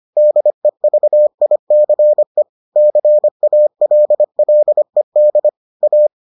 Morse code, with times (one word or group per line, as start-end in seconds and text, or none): 0.2-2.4 DEVICE
2.8-5.5 CALLED
5.8-6.1 A